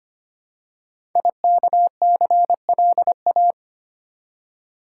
Morse code, heard as a keyed tone, about 25 wpm, 700 Hz. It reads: IKCLA
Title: Code Groups - 5 Letter:Numbers - 25wpm